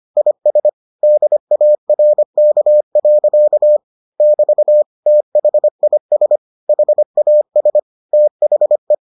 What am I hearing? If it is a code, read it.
Morse code, heard as a keyed tone, about 25 wpm, 600 Hz. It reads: IS DARK. <BT> THIS HAS THE